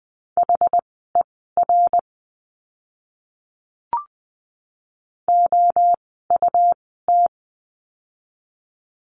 Morse code, heard as a keyed tone, about 20 wpm, spaced as for 15 wpm, 700 Hz. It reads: HER OUT
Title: 160 Lesson 26f - Letter g - Words 20wpm